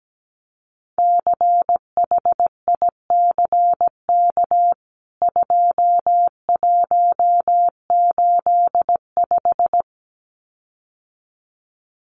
Morse code, heard as a keyed tone, about 17 wpm, 700 Hz. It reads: CHICK 2185
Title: CWT Exchange 17wpm